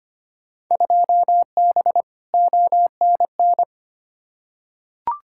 Morse code, heard as a keyed tone, about 25 wpm, 700 Hz. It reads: 2B ONN
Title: Field Day Exchanges Encore - 25wpm